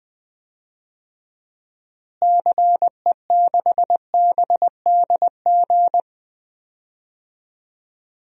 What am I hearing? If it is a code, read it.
Morse code, heard as a keyed tone, about 20 wpm, 700 Hz. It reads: CE6BDG
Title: DXCC-100 Call Signs - 6 Characters - Repeated 3x - 20wpm